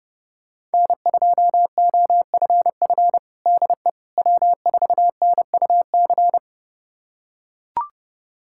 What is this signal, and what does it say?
Morse code, heard as a keyed tone, about 30 wpm, 700 Hz. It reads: N2OFF DE W4NUC
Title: USA Callsign DE Callsign - 5 Characters - 30wpm